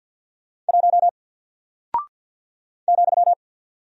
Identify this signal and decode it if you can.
Morse code, heard as a keyed tone, about 50 wpm, 700 Hz. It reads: 1 ,
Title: Single Character (Letter:Number:Punctuation:Prosign) Alt 50wpm